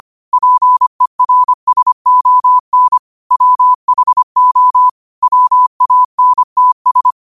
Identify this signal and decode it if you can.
Morse code, heard as a keyed tone, about 25 wpm, 1000 Hz. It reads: PERSON WHO WANTS